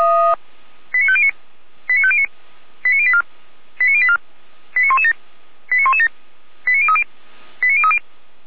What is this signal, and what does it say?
Five-tone selective call (ZVEI), heard as quick runs of five tones, tones of about 70 ms. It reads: 89509 89509 89084 89084 89117 89117 89030 89030
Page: /455/SDR/ZVEI/